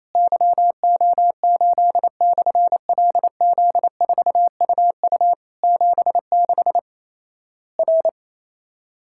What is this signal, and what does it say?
Morse code, heard as a keyed tone, about 28 wpm, 700 Hz. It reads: YO8/LZ4UU 76 R
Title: Worked All Europe DX Contest - QTC Exchange - 28wpm